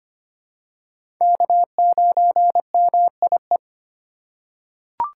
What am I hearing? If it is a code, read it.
Morse code, heard as a keyed tone, about 25 wpm, 700 Hz. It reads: K9MIE
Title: USA Call Signs - 4x Character Spacing - 25wpm